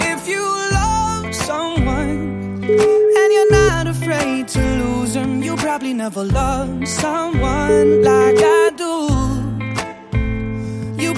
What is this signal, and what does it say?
Ringback tone (Europe, 425 Hz), with noise.